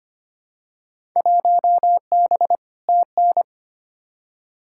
Morse code, heard as a keyed tone, about 25 wpm, 700 Hz. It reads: 1B TN
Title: Field Day Exchanges - 25wpm